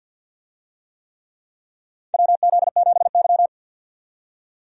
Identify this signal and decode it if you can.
Morse code, heard as a keyed tone, about 49 wpm, 700 Hz. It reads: WZ7Y
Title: USA Call Signs - 4 Characters - 50wpm